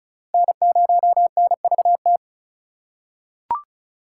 Morse code, heard as a keyed tone, about 36 wpm, 700 Hz. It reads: N0NVT
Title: USA Call Signs - 5 Characters - 2x Character Spacing - 35wpm